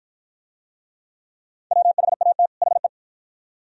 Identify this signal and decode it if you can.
Morse code, heard as a keyed tone, about 49 wpm, 700 Hz. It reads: WHAT HE